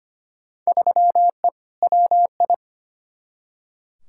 Morse code, heard as a keyed tone, about 25 wpm, 700 Hz. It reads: 3E WI
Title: Field Day Exchanges Encore - 25wpm